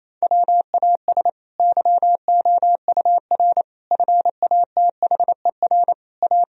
Morse code, heard as a keyed tone, about 28 wpm, 700 Hz. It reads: WAS YOUR FATHER A
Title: Sentences from Top 1000 Words - Review Part 4 - 28wpm